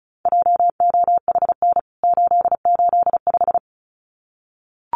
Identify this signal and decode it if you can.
Morse code, heard as a keyed tone, about 34 wpm, 700 Hz. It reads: JOHN 885
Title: CWT Exchange 35wpm